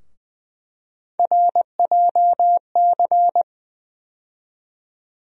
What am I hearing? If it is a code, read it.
Morse code, heard as a keyed tone, about 20 wpm, 700 Hz. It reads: RJC
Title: USA Call Sign Postfixes - 3 Letters 20wpm